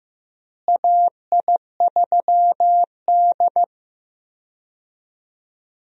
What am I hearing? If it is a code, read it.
Morse code, heard as a keyed tone, about 15 wpm, 700 Hz. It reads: AI3D
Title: USA Call Signs - 4 Characters - Repeated 3x - 15wpm